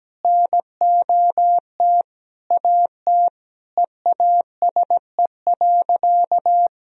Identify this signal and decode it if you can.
Morse code, heard as a keyed tone, about 17 wpm, 700 Hz. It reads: NOT AT EASE.
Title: Sentences from Top 1000 Words - Review Part 1 - 17wpm